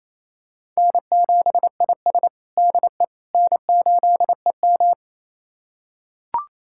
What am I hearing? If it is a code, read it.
Morse code, heard as a keyed tone, about 28 wpm, 700 Hz. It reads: N7IS DE N8EM
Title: USA Callsign DE Callsign - 4 Characters - 28wpm